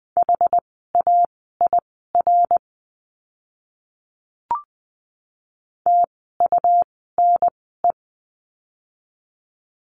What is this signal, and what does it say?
Morse code, heard as a keyed tone, about 20 wpm, spaced as for 14 wpm, 700 Hz. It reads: HAIR TUNE
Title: 101 Lesson 17f - Letter u - Words 20wpm